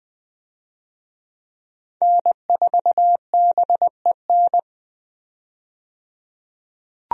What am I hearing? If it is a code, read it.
Morse code, heard as a keyed tone, about 20 wpm, 700 Hz. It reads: N4BEN
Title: USA Call Signs - 5 Characters - 20wpm